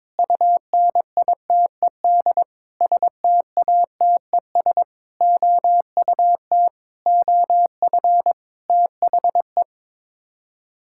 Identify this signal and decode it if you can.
Morse code, heard as a keyed tone, about 22 wpm, 700 Hz. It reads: UNITED STATES OUT OF THE
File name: practice / 22wpm / Sets of 8 Words - 22wpm